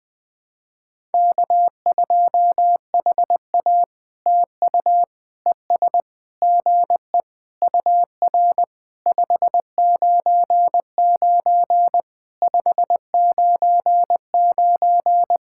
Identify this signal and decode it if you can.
Morse code, heard as a keyed tone, about 20 wpm, 700 Hz. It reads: K2HA TU ES GE UR 599 599
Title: POTA QSO Practice - 20wpm